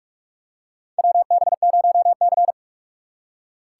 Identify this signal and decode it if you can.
Morse code, heard as a keyed tone, about 42 wpm, 700 Hz. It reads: WB0C
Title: USA Call Signs - 6x Character Spacing - 45wpm